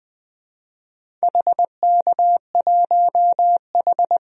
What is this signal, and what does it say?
Morse code, heard as a keyed tone, about 20 wpm, 700 Hz. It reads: HK1H
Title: DXCC-100 Call Signs - 4 Characters - 20wpm